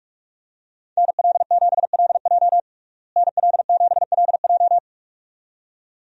Morse code, heard as a keyed tone, about 46 wpm, 700 Hz. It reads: NL7LJ NL7LJ
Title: USA Call Signs - 5 Characters - Repeated 2x - 45wpm